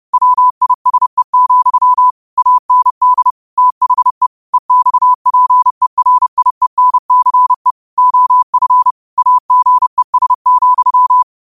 Morse code, heard as a keyed tone, about 30 wpm, 1000 Hz. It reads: WISE, AND THE EXPERIENCE OF AGES,
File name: qotdfiles / 30wpm